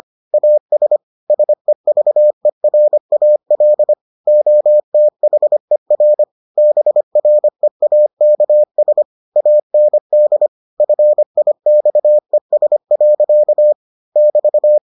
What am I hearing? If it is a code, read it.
Morse code, heard as a keyed tone, about 25 wpm, 600 Hz. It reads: AS SEVERAL OTHER BREAKS AND FIXES. <BT>